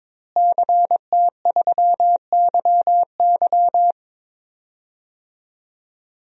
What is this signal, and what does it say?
Morse code, heard as a keyed tone, about 21 wpm, 700 Hz. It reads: CT3YY
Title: DXCC-100 Call Signs - 5 Characters - Repeated 3x - 22wpm